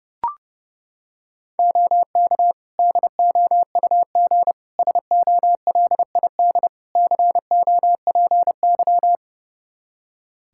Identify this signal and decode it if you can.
Morse code, heard as a keyed tone, about 32 wpm, 700 Hz. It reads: OK DOUG SOLID COPY
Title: Practice QSO - Exchange 3 - More Info - 30wpm